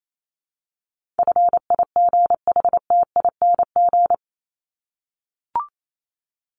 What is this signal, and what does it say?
Morse code, heard as a keyed tone, about 28 wpm, 700 Hz. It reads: FIGHTING
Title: ING Suffix - Top 100 Words with Base Words 28wpm